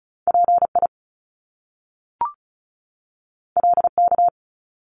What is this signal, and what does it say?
Morse code, heard as a keyed tone, about 36 wpm, 700 Hz. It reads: PI LK
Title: Top Two Letter Combinations in Words 35wpm